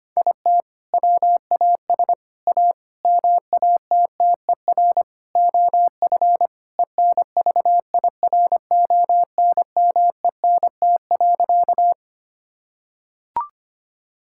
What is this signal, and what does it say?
Morse code, heard as a keyed tone, about 25 wpm, 700 Hz. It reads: IT WAS A MATTER OF ENVIRONMENT.
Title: Sentences from Top 1000 Words - Review Part 5 - 25wpm